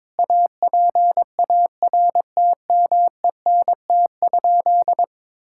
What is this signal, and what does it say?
Morse code, heard as a keyed tone, about 21 wpm, 700 Hz. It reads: APARTMENT?
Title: Sentences from Top 1000 Words - Review Part 6 - 22wpm